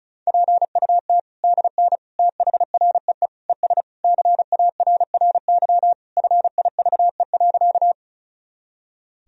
Morse code, heard as a keyed tone, about 36 wpm, 700 Hz. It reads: PUT DN THREE ES CARRY FIVE.
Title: Sentences from Top 400 Words and Abbreviations 35wpm